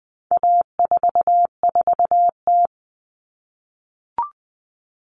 Morse code, heard as a keyed tone, about 20 wpm, 700 Hz. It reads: A44T
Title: All Cut Numbers - 4-Digit 20wpm